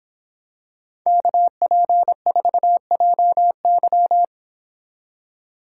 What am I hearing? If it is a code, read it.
Morse code, heard as a keyed tone, about 26 wpm, 700 Hz. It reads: KP4JY